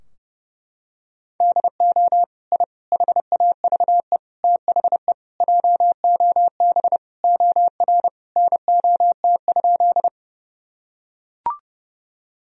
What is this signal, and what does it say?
Morse code, heard as a keyed tone, about 30 wpm, 700 Hz. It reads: DO I HAVE THE JOB OR NOT?
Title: Sentences from Top 1000 Words - Review Part 3 - 30wpm